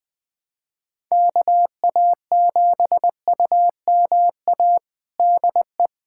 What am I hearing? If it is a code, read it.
Morse code, heard as a keyed tone, about 20 wpm, 700 Hz. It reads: KA7UMA DE
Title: USA Callsign DE Callsign - 6 Characters - 20wpm